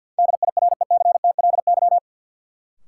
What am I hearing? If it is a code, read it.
Morse code, heard as a keyed tone, about 51 wpm, 700 Hz. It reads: DIRECTLY
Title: Sentences from Top 800 Words - New Word Drill - 50wpm